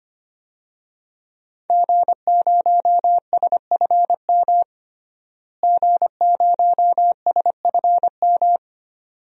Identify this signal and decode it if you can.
Morse code, heard as a keyed tone, about 25 wpm, 700 Hz. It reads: G0SFM G0SFM
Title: DXCC-100 Call Signs - 5 Characters - Repeated 2x - 25wpm